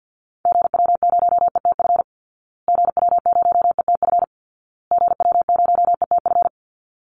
Morse code, heard as a keyed tone, about 50 wpm, 700 Hz. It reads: GW0ETF GW0ETF GW0ETF
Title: Call Signs Repeated 2x - 50wpm